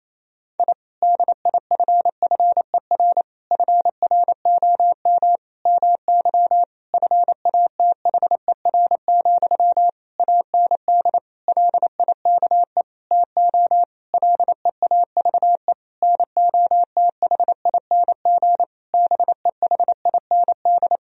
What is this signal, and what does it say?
Morse code, heard as a keyed tone, about 28 wpm, 700 Hz. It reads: I DIFFER FROM MY FATHER, AND LIKE TO LEAVE NOTHING BEHIND
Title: Sentences from Top 1000 Words - Review Part 1 - 28wpm